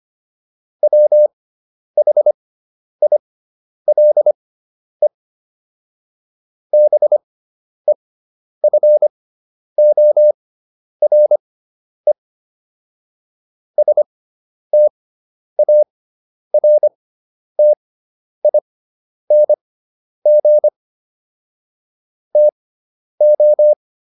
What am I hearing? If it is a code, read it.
Morse code, heard as a keyed tone, about 25 wpm, spaced as for 10 wpm, 600 Hz. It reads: WHILE BEFORE STARTING TO